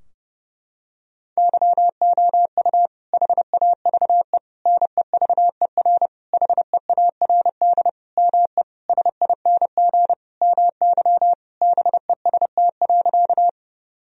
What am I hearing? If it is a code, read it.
Morse code, heard as a keyed tone, about 30 wpm, 700 Hz. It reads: YOU HAVE NEVER HEARD ME SING MY BEST.
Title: Sentences from Top 1000 Words - Review Part 2 - 30wpm